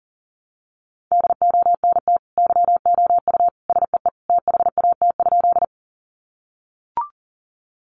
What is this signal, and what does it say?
Morse code, heard as a keyed tone, about 40 wpm, 700 Hz. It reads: DONT YOU SEE THAT?
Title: Sentences from Top 300 Words 40wpm